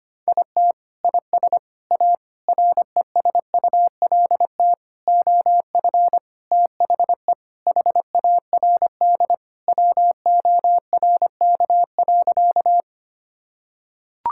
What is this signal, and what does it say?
Morse code, heard as a keyed tone, about 25 wpm, 700 Hz. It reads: IT IS A RESULT OF THE HARD WORK.
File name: practice / 25wpm / Sentences with 4-Word Phrases - 25wpm